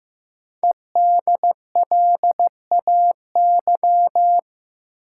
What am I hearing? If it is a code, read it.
Morse code, heard as a keyed tone, about 15 wpm, 700 Hz. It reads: EDLAY
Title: Code Groups - 5 Letters - 15wpm